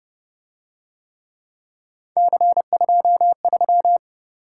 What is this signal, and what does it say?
Morse code, heard as a keyed tone, about 28 wpm, 700 Hz. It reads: C23